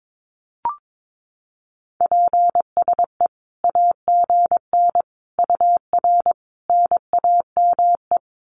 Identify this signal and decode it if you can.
Morse code, heard as a keyed tone, about 22 wpm, 700 Hz. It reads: PSE AGN UR NAME
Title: Common QSO Phrases 22wpm